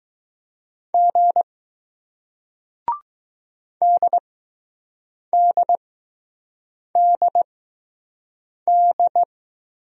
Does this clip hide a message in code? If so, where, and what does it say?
Yes: Morse code from 0.9 to 9.2 s, heard as a keyed tone, about 20 wpm, spaced as for 6 wpm, 700 Hz. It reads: G DDDD